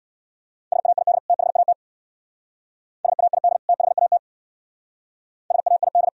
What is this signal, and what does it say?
Morse code, heard as a keyed tone, about 64 wpm, 700 Hz. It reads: SAID THAT SAID THAT SAID